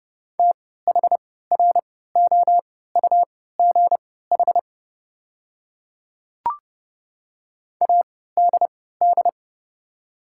Morse code, heard as a keyed tone, about 30 wpm, spaced as for 17 wpm, 690 Hz. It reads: THROUGH ADD
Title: 3x Character Spacing - Top 200 Words - 30wpm